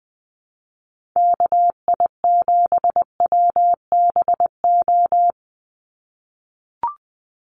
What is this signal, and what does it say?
Morse code, heard as a keyed tone, about 20 wpm, 700 Hz. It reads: KI7WBO
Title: USA Call Signs 20wpm